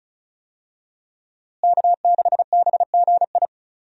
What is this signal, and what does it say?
Morse code, heard as a keyed tone, about 34 wpm, 700 Hz. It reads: K6BGI
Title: USA Call Signs - 5 Characters - Repeated 2x - 35wpm